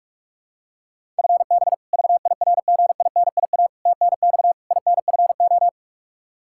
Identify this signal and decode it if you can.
Morse code, heard as a keyed tone, about 42 wpm, 700 Hz. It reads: FB VIRGINIA TNX INFO
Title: K1USN SST QSO Practice 45wpm